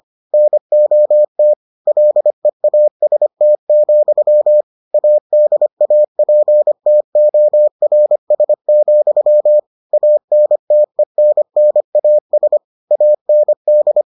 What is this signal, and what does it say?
Morse code, heard as a keyed tone, about 25 wpm, 600 Hz. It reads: NOT LEAST, ADAPTORS, ANTENNAS AND